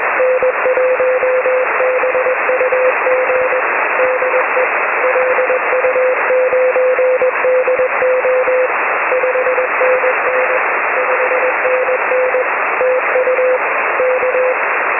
Morse code, heard as a keyed tone, about 19 wpm, 510 Hz, with radio noise. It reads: N1BUG DE LU9DO 5NN 5NN TU K
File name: 6m-lu9do